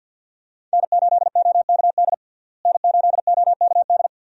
Morse code, heard as a keyed tone, about 48 wpm, 700 Hz. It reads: N8OKD N8OKD
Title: USA Call Signs - 5 Characters - Repeated 2x - 50wpm